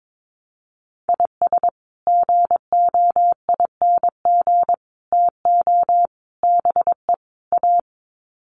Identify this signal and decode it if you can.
Morse code, heard as a keyed tone, about 22 wpm, 700 Hz. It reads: IS GOING TO BE A
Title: Sets of 5 Words 22wpm